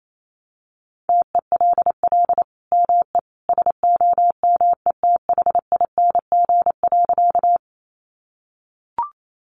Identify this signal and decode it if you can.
Morse code, heard as a keyed tone, about 28 wpm, 700 Hz. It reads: TELL ME SOMETHING.
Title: Sentences from Top 600 Words 28wpm